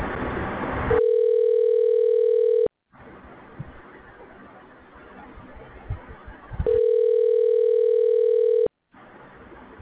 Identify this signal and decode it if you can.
Ringback tone (North America), 440 + 480 Hz.